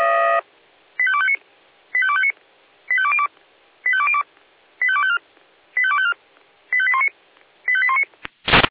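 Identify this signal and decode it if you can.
Five-tone selective call (ZVEI), heard as quick runs of five tones, tones of about 70 ms. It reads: 85259 85259 85202 85202 85254 85254 85819 85819
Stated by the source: ZVEI five-tone selective call received on SDR.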